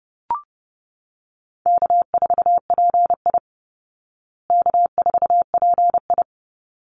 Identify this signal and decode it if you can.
Morse code, heard as a keyed tone, about 30 wpm, 700 Hz. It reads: K4PI K4PI
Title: Call Signs Repeated 3x - 30wpm